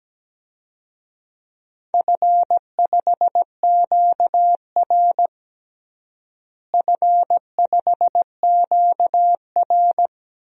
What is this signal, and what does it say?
Morse code, heard as a keyed tone, about 17 wpm, 700 Hz. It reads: F5QR F5QR
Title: DXCC-100 Call Signs - 4 Characters - Repeated 2x - 17wpm